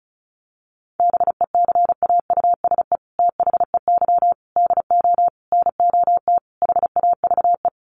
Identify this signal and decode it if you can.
Morse code, heard as a keyed tone, about 34 wpm, 700 Hz. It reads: BECAUSE THEY DO NOT HAVE
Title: Sets of 5 Words Encore2 35wpm